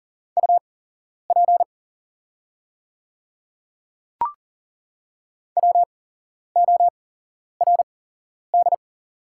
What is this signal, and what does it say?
Morse code, heard as a keyed tone, about 40 wpm, spaced as for 11 wpm, 700 Hz. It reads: UP WORD
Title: 8x Character Spacing - Top 100 Words - 40wpm